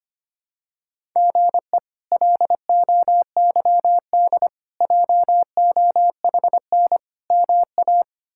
Morse code, heard as a keyed tone, about 25 wpm, 700 Hz. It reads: GE LOYD JOHN MA
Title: K1USN SST QSO Practice - 25wpm